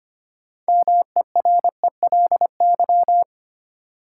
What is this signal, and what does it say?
Morse code, heard as a keyed tone, about 25 wpm, 700 Hz. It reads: MERELY
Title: Sentences from Top 1400 Words - New Word Drill - 25wpm